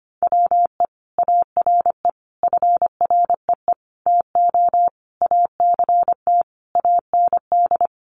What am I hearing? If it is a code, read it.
Morse code, heard as a keyed tone, about 25 wpm, 700 Hz. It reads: WE ARE FREE TO ACT AND
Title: Sentences from Top 500 Words 25wpm